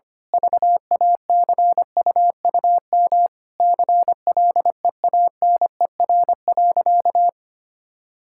Morse code, heard as a keyed tone, about 25 wpm, 700 Hz. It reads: VACUUM CLEANER.
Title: Pangram Challenge v2 - 25wpm